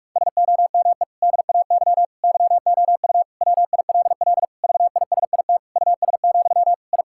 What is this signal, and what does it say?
Morse code, heard as a keyed tone, about 46 wpm, 700 Hz. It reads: SOME DAY YOU WILL VISIT US, I
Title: Sentences from Top 1000 Words - Review Part 6 - 45wpm